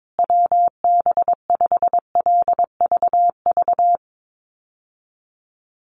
Morse code, heard as a keyed tone, about 22 wpm, 700 Hz. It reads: WB5LVV
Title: USA Call Signs Repeated 2x 22wpm